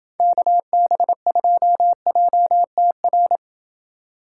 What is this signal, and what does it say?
Morse code, heard as a keyed tone, about 28 wpm, 700 Hz. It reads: KB2JTR